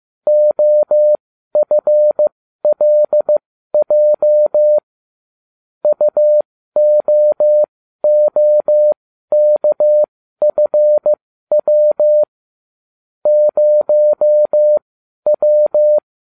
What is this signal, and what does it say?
Morse code, heard as a keyed tone, about 15 wpm, spaced as for 12 wpm, 600 Hz. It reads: OFLJ UOOKFW 0W